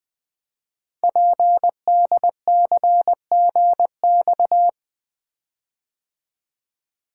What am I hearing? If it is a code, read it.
Morse code, heard as a keyed tone, about 20 wpm, 700 Hz. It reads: PDCGX
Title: Code Groups - 5 Letters - 20wpm